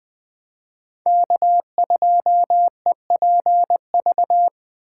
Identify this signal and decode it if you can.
Morse code, heard as a keyed tone, about 21 wpm, 700 Hz. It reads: K2EPV
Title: USA Call Signs - 5 Characters - 2x Character Spacing - 20wpm